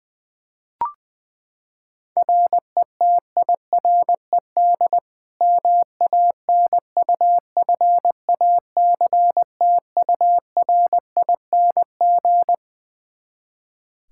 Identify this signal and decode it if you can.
Morse code, heard as a keyed tone, about 20 wpm, 700 Hz. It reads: RETIRED MANUFACTURING
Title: Practice QSO - Exchange 3 - More Info - 20wpm